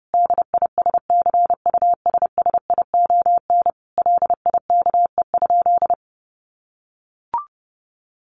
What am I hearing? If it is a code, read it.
Morse code, heard as a keyed tone, about 28 wpm, 700 Hz. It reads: DISCUSSION LIKE?
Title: Sentences from Top 1500 Words 30wpm